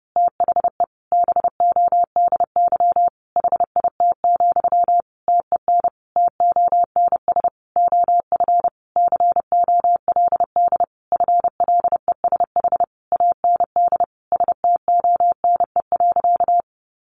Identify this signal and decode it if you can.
Morse code, heard as a keyed tone, about 30 wpm, 700 Hz. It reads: THE BODY HIT, TEN TONS OF COLD FLESH AND STONE.
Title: A Sound Of Thunder 30wpm